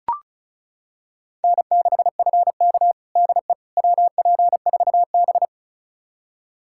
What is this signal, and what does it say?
Morse code, heard as a keyed tone, about 36 wpm, 700 Hz. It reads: N6FK DE WP4B
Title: USA Callsign DE Callsign - 4 Characters - 35wpm